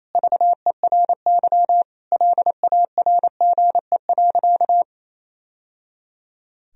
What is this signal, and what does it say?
Morse code, heard as a keyed tone, about 28 wpm, 700 Hz. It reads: VERY LARGE.
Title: Sentences from Top 1000 Words - Review Part 1 - 28wpm